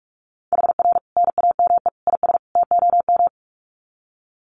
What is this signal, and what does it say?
Morse code, heard as a keyed tone, about 44 wpm, 700 Hz. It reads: HR NAME IS TOM
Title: Common QSO Phrases 45wpm